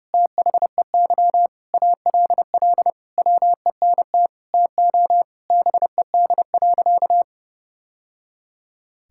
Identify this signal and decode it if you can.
Morse code, heard as a keyed tone, about 30 wpm, 700 Hz. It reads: THEY ALL WENT TO BED.
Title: Sentences from Top 1000 Words - Review Part 6 - 30wpm